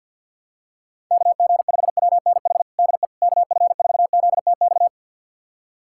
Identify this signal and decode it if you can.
Morse code, heard as a keyed tone, about 53 wpm, 700 Hz. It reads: KG5WNH DE KF4ZTX
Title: USA Callsign DE Callsign 50wpm